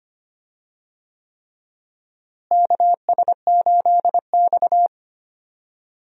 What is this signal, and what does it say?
Morse code, heard as a keyed tone, about 25 wpm, 700 Hz. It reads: KS8X
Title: USA Call Signs - 4 Characters - Repeated 3x - 25wpm